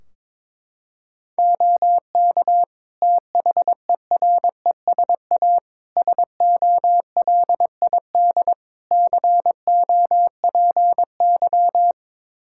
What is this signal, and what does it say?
Morse code, heard as a keyed tone, about 22 wpm, 700 Hz. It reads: OK THERESA SOLID COPY
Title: Practice QSO - Exchange 3 - More Info - 22wpm